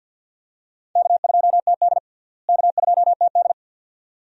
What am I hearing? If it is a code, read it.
Morse code, heard as a keyed tone, about 46 wpm, 700 Hz. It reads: K2TD K2TD
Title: USA Call Signs - 4 Characters - Repeated 2x - 50wpm